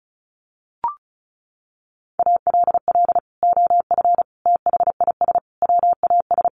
Morse code, heard as a keyed tone, about 34 wpm, 700 Hz. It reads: ALL OF THIS WAS
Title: Sentences from Top 500 Words 35wpm